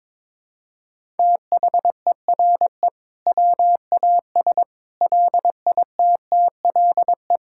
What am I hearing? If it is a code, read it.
Morse code, heard as a keyed tone, about 21 wpm, 700 Hz. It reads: THERE WAS LITTLE
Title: Sentences from Top 1000 Words - Review Part 5 - 22wpm